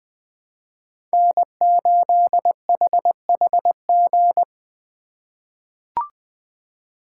Morse code, heard as a keyed tone, about 20 wpm, 700 Hz. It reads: N8HHG
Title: K1USN SST QSO Practice - 20wpm